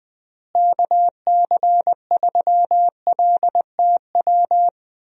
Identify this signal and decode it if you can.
Morse code, heard as a keyed tone, about 20 wpm, 700 Hz. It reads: KC3LTW
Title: USA Call Signs - 6 Characters - Repeated 3x - 20wpm